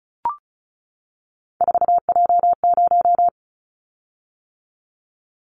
Morse code, heard as a keyed tone, about 34 wpm, 700 Hz. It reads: 4J0
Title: Callsign Prefixes for Top 100 DXCC - 35wpm